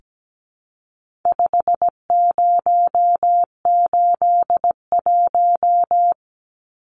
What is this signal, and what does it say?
Morse code, heard as a keyed tone, about 17 wpm, 700 Hz. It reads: 5081
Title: Numbers - 4-Digit 17wpm